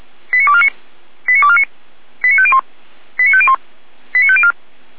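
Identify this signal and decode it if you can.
Five-tone selective call (ZVEI), heard as quick runs of five tones, tones of about 70 ms. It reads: 89249 89249 89501 89501 89504
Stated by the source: ZVEI five-tone selective call received on an SDR by the Feuerwehr Orlamünde fire brigade.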